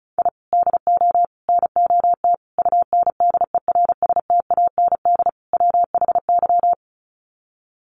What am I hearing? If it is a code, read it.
Morse code, heard as a keyed tone, about 34 wpm, 700 Hz. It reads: I DO NOT UNDERSTAND WHY
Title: Sets of 5 Words Encore 35wpm